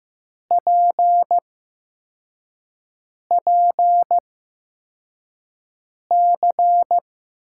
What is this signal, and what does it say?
Morse code, heard as a keyed tone, about 15 wpm, spaced as for 4 wpm, 700 Hz. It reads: PPC
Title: USA Call Signs - 5 Characters - 8x Character Spacing - 15wpm